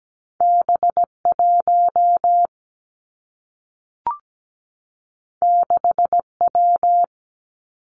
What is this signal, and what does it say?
Morse code, heard as a keyed tone, about 17 wpm, 700 Hz. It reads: B1 6W
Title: Two Challenging Letter-Number Combinations 17wpm